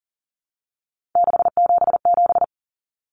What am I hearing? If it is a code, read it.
Morse code, heard as a keyed tone, about 40 wpm, 700 Hz. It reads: 677
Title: Numbers - 3-Digit 40wpm